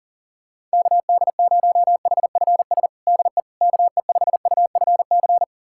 Morse code, heard as a keyed tone, about 40 wpm, 700 Hz. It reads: KD0HFS DE KE5UFC
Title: USA Callsign DE Callsign - 6 Characters - 40wpm